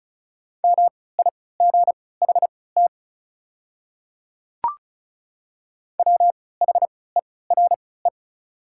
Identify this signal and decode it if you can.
Morse code, heard as a keyed tone, about 34 wpm, spaced as for 20 wpm, 700 Hz. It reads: MIGHT WHERE
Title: 3x Character Spacing - Top 300 Words - 35wpm